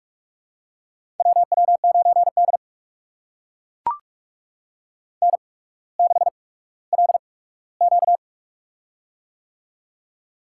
Morse code, heard as a keyed tone, about 44 wpm, spaced as for 19 wpm, 700 Hz. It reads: WW9D N6LQ
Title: USA Call Signs - 4 Characters - 8x Character Spacing - 45wpm